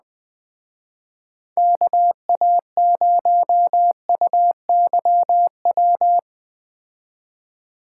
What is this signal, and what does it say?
Morse code, heard as a keyed tone, about 19 wpm, 700 Hz. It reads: KA0UYW
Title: Speed Racing - USA 2x3 Call Signs 17wpm